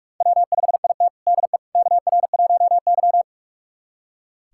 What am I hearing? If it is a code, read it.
Morse code, heard as a keyed tone, about 43 wpm, 700 Hz. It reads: W5IT DE KR1Y
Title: USA Callsign DE Callsign - 4 Characters - 45wpm